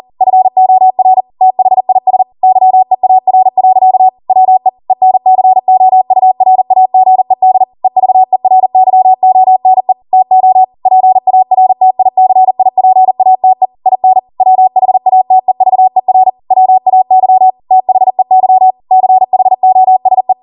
Morse code, heard as a keyed tone, about 40 wpm, 750 Hz. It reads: FOR THIS YEAR. WE ENCOURAGED EVERYONE TO PARTICIPATE IN WHATEVER WAY THEY CHOSE